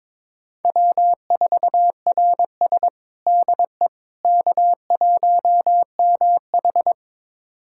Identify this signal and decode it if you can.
Morse code, heard as a keyed tone, about 22 wpm, 700 Hz. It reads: W4RS DE K1MH
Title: USA Callsign DE Callsign - 4 Characters - 22wpm